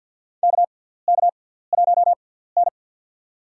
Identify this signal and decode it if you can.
Morse code, heard as a keyed tone, about 47 wpm, spaced as for 17 wpm, 700 Hz. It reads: KK1N
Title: USA Call Signs - 4 Characters - 6x Character Spacing - 50wpm